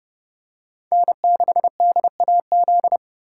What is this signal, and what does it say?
Morse code, heard as a keyed tone, about 29 wpm, 700 Hz. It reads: N6DAZ